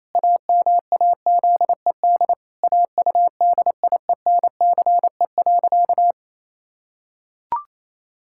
Morse code, heard as a keyed tone, about 27 wpm, 700 Hz. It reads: AMAZED AUDIENCE.
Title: Pangram Challenge v2 - 28wpm